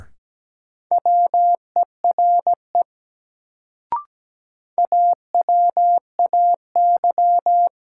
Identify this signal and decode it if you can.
Morse code, heard as a keyed tone, about 17 wpm, 700 Hz. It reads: WERE AWAY
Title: Sets of 2 Words - New Word Drill - 17wpm